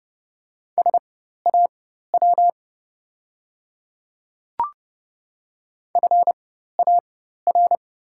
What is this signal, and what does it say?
Morse code, heard as a keyed tone, about 31 wpm, spaced as for 14 wpm, 690 Hz. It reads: SAW FAR
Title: 4x Character Spacing - Top 500 Words - 30wpm